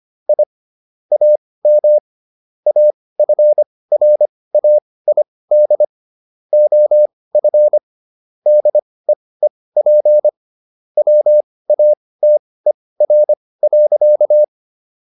Morse code, heard as a keyed tone, about 25 wpm, spaced as for 18 wpm, 600 Hz. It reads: I AM AFRAID OF DEEP WATER.